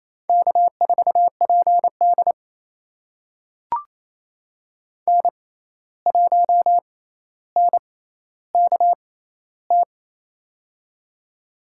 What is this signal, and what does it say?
Morse code, heard as a keyed tone, about 28 wpm, spaced as for 13 wpm, 700 Hz. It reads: K4PD N1NKT
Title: USA Call Signs - 6x Character Spacing - 28wpm